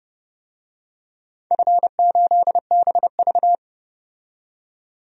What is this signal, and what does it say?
Morse code, heard as a keyed tone, about 30 wpm, 700 Hz. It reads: F8BV
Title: DXCC-100 Call Signs - 4 Characters - 30wpm